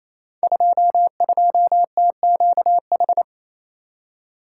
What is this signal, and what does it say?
Morse code, heard as a keyed tone, about 28 wpm, 700 Hz. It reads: 22TQH
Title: Code Groups - 5 Letter:Numbers - 28wpm